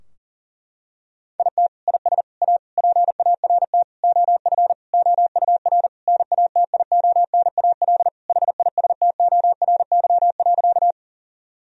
Morse code, heard as a keyed tone, about 37 wpm, 700 Hz. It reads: IT IS A PART OF OUR NATIONAL HISTORY.
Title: Sentences from Top 1000 Words - Review Part 4 - 40wpm